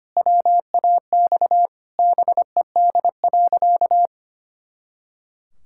Morse code, heard as a keyed tone, about 25 wpm, 700 Hz. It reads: WAX BED.
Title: Pangram Challenge v2 - 25wpm